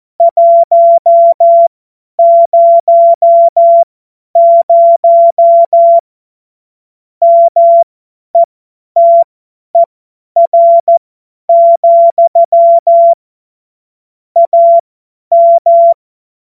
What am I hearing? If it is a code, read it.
Morse code, heard as a keyed tone, about 14 wpm, spaced as for 10 wpm, 680 Hz. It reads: 100 METER, AM
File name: Morsekurs / Witze / Witz 30. Dezember 2025 50 F70